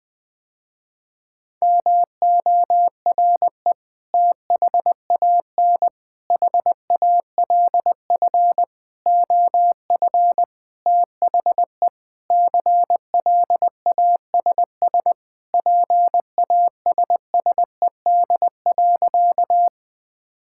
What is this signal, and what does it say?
Morse code, heard as a keyed tone, about 20 wpm, 700 Hz. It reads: MORE THAN HALF OF THE CLASS PASSED.
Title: Sentences with 5-Word Phrases - Spoken Beginning - 20wpm